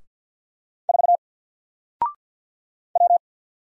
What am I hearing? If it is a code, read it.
Morse code, heard as a keyed tone, about 50 wpm, 700 Hz. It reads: 4 W
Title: Single Letter-Number Alt 50wpm